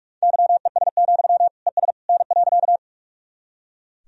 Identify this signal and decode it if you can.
Morse code, heard as a keyed tone, about 43 wpm, 690 Hz. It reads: YES, ES N.
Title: Sentences from Top 500 Words and Abbreviations 45wpm